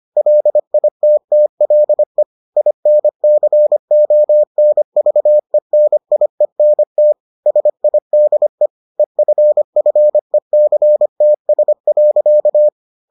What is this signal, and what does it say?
Morse code, heard as a keyed tone, about 25 wpm, 600 Hz. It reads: LITTLE INCONVENIENT SIDE EFFECTS.